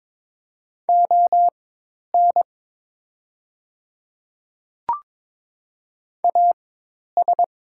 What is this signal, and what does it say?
Morse code, heard as a keyed tone, about 22 wpm, spaced as for 10 wpm, 690 Hz. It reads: ON AS
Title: 4x Character Spacing - Top 200 Words - 22wpm